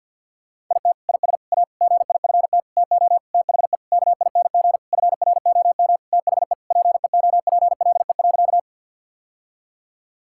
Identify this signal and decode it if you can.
Morse code, heard as a keyed tone, about 50 wpm, 700 Hz. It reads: IT IS A GIFT TO THE KING FROM THE PEOPLE.
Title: Sentences from Top 1000 Words - Review Part 6 - 50wpm